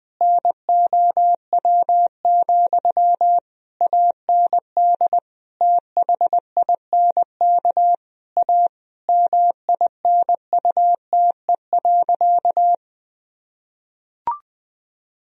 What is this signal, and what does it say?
Morse code, heard as a keyed tone, about 20 wpm, 700 Hz. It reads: NOW, AND THINK A MINUTE.
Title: Sentences from Top 1000 Words - Review Part 1 - 20wpm